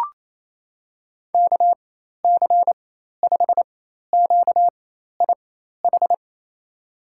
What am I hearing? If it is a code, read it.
Morse code, heard as a keyed tone, about 28 wpm, spaced as for 13 wpm, 700 Hz. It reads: KC5QIH
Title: USA Call Signs - 4x Character Spacing - 28wpm